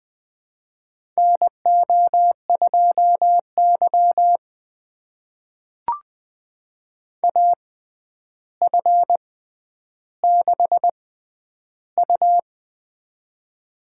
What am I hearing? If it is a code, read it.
Morse code, heard as a keyed tone, about 20 wpm, spaced as for 10 wpm, 700 Hz. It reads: NO2Y AF6U
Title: USA Call Signs - 4 Characters - 6x Character Spacing - 20wpm